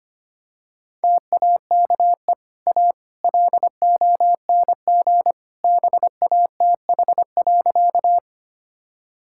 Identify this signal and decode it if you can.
Morse code, heard as a keyed tone, about 25 wpm, 690 Hz. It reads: TAKE A LONG BATH.
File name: Easy Sentences with Hints - 25wpm